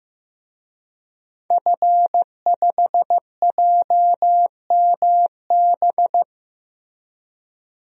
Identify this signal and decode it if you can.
Morse code, heard as a keyed tone, about 15 wpm, 700 Hz. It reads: F5JMB
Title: DXCC-100 Call Signs - 5 Characters - Repeated 2x - 15wpm